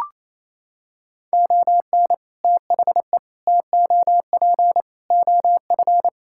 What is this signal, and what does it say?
Morse code, heard as a keyed tone, about 28 wpm, 700 Hz. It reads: ON THE TOP OF
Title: Sentences from Top 1000 Words - Review Part 6 - 28wpm